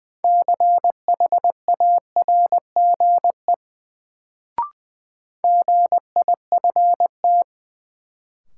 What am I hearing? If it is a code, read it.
Morse code, heard as a keyed tone, about 20 wpm, 700 Hz. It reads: CHARGE GIFT
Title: Sentences from Top 900 Words - New Word Drill - 20wpm